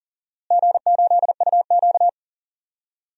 Morse code, heard as a keyed tone, about 40 wpm, 700 Hz. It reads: G8UQ
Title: DXCC-100 Call Signs - 4 Characters - Repeated 2x - 40wpm